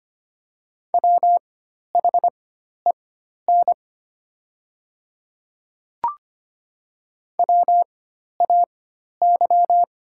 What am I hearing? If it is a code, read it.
Morse code, heard as a keyed tone, about 25 wpm, spaced as for 12 wpm, 700 Hz. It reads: WHEN WAY
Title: 4x Character Spacing - Top 100 Words - 25wpm